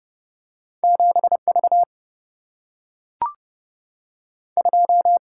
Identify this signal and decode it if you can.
Morse code, heard as a keyed tone, about 30 wpm, 700 Hz. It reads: 7V 2
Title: Single-Letter Single-Number 30wpm